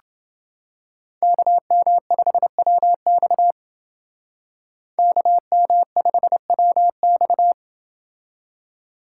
Morse code, heard as a keyed tone, about 29 wpm, 700 Hz. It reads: KM5WX KM5WX